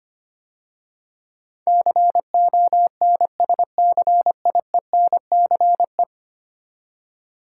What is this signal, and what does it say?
Morse code, heard as a keyed tone, about 25 wpm, 700 Hz. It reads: CONSCIENCE
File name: 10 Letter Words - First 3 Letters Given - 25wpm